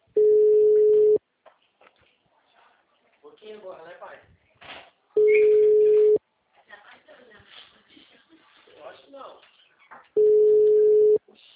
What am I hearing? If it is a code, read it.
Ringback tone (Europe, 425 Hz), with speech.
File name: out- -2017- - - .8